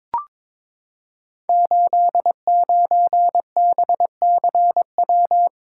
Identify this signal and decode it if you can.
Morse code, heard as a keyed tone, about 22 wpm, 700 Hz. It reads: 89BCW
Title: Code Groups - 5 Letter:Numbers - 22wpm